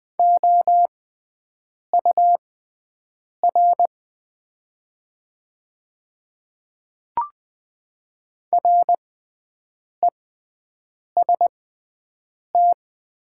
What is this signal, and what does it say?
Morse code, heard as a keyed tone, about 20 wpm, spaced as for 7 wpm, 700 Hz. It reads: OUR REST
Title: 6x Character Spacing - Top 500 Words - 20wpm